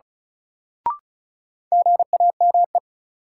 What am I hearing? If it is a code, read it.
Morse code, heard as a keyed tone, about 36 wpm, 700 Hz. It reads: GAME
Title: Sentences from Top 500 Words - New Word Drill - 35wpm